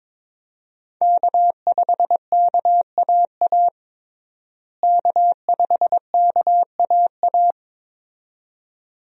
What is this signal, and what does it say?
Morse code, heard as a keyed tone, about 22 wpm, 700 Hz. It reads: K5KAA K5KAA
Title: USA Call Signs - 5 Characters - Repeated 2x - 22wpm